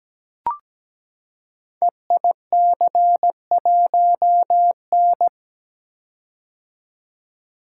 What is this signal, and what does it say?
Morse code, heard as a keyed tone, about 17 wpm, 700 Hz. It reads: EIC1N
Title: Code Groups - 5 Letter:Numbers - 17wpm